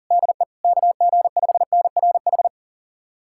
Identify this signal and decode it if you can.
Morse code, heard as a keyed tone, about 40 wpm, 700 Hz. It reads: DE KG5NRH
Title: USA Callsign DE Callsign - 6 Characters - 40wpm